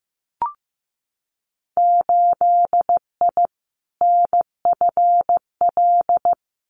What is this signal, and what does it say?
Morse code, heard as a keyed tone, about 15 wpm, 700 Hz. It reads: 8I NFL
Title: Winter Field Day Exchanges Encore 15wpm